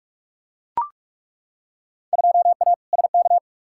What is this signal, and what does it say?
Morse code, heard as a keyed tone, about 42 wpm, 700 Hz. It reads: 2A SK
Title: Field Day Exchanges Encore - 45wpm